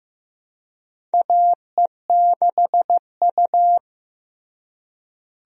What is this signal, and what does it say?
Morse code, heard as a keyed tone, about 16 wpm, 700 Hz. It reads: AE6U
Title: USA Call Signs - 4 Characters - Repeated 2x - 15wpm